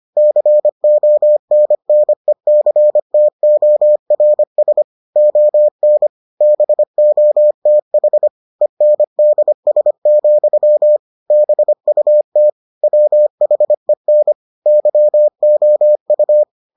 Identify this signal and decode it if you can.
Morse code, heard as a keyed tone, about 25 wpm, 600 Hz. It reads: CONNECTORS ON BOTH ENDS, BUT WHEN YOU